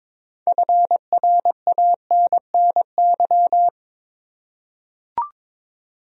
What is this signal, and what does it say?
Morse code, heard as a keyed tone, about 21 wpm, 700 Hz. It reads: FRANNY
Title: USA Common Names - 22wpm